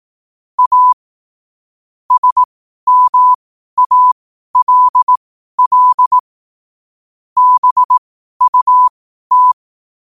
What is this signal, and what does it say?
Morse code, heard as a keyed tone, about 18 wpm, spaced as for 12 wpm, 1000 Hz. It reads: A SMALL BUT
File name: qotdfiles / 12wpm